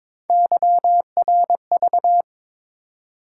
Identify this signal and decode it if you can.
Morse code, heard as a keyed tone, about 27 wpm, 700 Hz. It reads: YRV
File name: Speed-Racing - Three Letter-Number Combinations - 22wpm